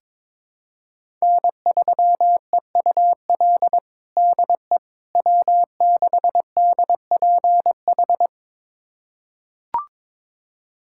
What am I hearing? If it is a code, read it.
Morse code, heard as a keyed tone, about 22 wpm, 700 Hz. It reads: N3EUL DE W6DPH
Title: USA Callsign DE Callsign - 5 Characters - 22wpm